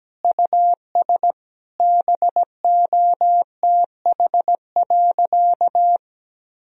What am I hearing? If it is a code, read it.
Morse code, heard as a keyed tone, about 17 wpm, 700 Hz. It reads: US BOTH.
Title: Sentences from Top 1000 Words - Review Part 4 - 17wpm